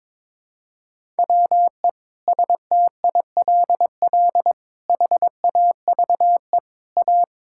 Morse code, heard as a keyed tone, about 21 wpm, 700 Hz. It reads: WE STILL HAVE A
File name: Sentences with 7-Word Phrases - 22wpm